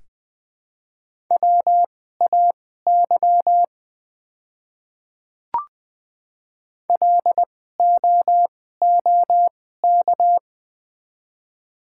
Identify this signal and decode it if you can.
Morse code, heard as a keyed tone, about 20 wpm, spaced as for 15 wpm, 700 Hz. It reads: WAY LOOK
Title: 2x Character Spacing - Top 100 Words - 20wpm